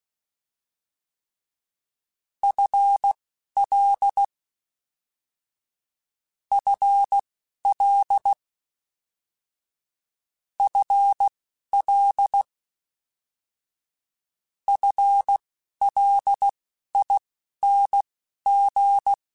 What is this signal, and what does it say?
Morse code, heard as a keyed tone, about 16 wpm, spaced as for 12 wpm, 800 Hz. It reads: FL FL FL FLING
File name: FL / FL mp3s / FL Prefixes 3x2 Twice @10